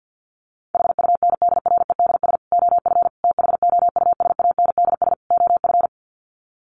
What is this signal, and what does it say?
Morse code, heard as a keyed tone, about 50 wpm, 700 Hz. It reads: HUNDREDS OF THOUSANDS OF
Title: Sets of 4 Words 50wpm